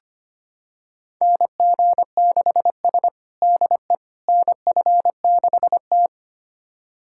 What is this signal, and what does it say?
Morse code, heard as a keyed tone, about 25 wpm, 700 Hz. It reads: NG6S DE NF6T
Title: USA Callsign DE Callsign - 4 Characters - 25wpm